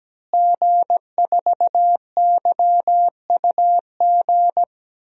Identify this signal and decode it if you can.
Morse code, heard as a keyed tone, about 17 wpm, 700 Hz. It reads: G4YUG
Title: DXCC-100 Call Signs - 5 Characters - Repeated 3x - 17wpm